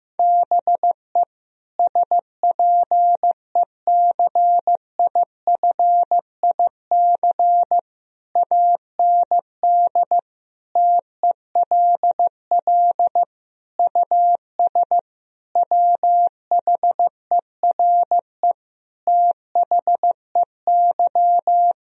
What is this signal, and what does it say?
Morse code, heard as a keyed tone, about 16 wpm, 700 Hz. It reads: BE SPECIFIC AND TELL US WHERE THEY
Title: Sentences from Top 1000 Words - Review Part 3 - 15wpm